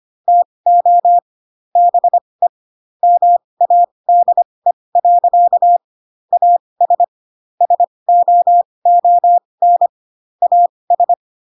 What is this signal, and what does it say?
Morse code, heard as a keyed tone, about 25 wpm, spaced as for 20 wpm, 700 Hz. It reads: TO BE MADE. AS SOON AS